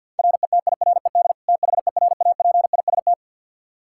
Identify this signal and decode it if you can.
Morse code, heard as a keyed tone, about 52 wpm, 700 Hz. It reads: RETIRED THERAPIST
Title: Practice QSO - Exchange 3 - More Info - 50wpm